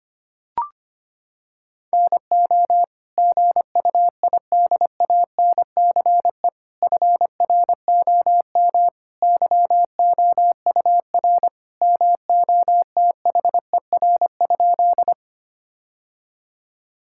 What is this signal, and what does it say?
Morse code, heard as a keyed tone, about 25 wpm, 700 Hz. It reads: NO GUIDANCE FROM YOUR MOTHER?
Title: Sentences from Top 1600 Words 25wpm